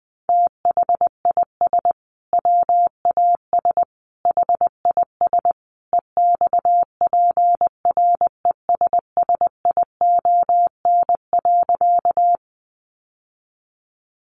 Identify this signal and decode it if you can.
Morse code, heard as a keyed tone, about 20 wpm, 700 Hz. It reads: THIS WAS HIS EXPRESSION.
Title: Sentences from Top 1200 Words 20wpm